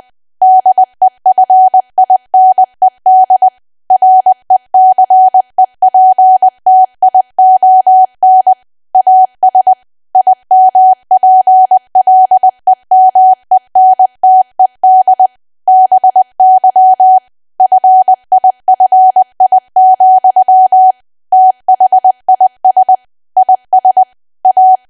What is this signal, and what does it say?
Morse code, heard as a keyed tone, about 20 wpm, 750 Hz. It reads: DEFINED RECEPTION AS IMPLEMENTED BY FIFI, THIS IS A
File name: cwa / W1AW-Level3 / 20WPM